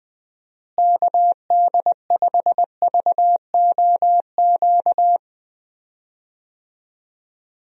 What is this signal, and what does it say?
Morse code, heard as a keyed tone, about 20 wpm, 700 Hz. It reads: KD5VOQ
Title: USA Call Signs - 6 Characters - Repeated 2x - 20wpm